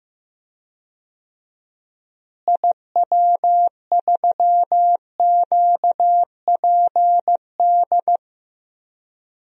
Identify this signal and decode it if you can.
Morse code, heard as a keyed tone, about 15 wpm, 700 Hz. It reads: IW3QPD
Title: DXCC-100 Call Signs - 6 Characters - Repeated 3x - 15wpm